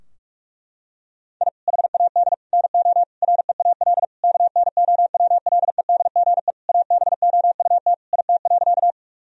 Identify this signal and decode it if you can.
Morse code, heard as a keyed tone, about 44 wpm, 700 Hz. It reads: I HAD NO REAL KNOWLEDGE ABOUT IT.
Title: Sentences from Top 1000 Words - Review Part 3 - 45wpm